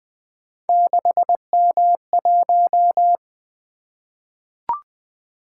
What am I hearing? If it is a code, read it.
Morse code, heard as a keyed tone, about 20 wpm, 700 Hz. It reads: 6M1
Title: Three Letter-Number Combinations - 20wpm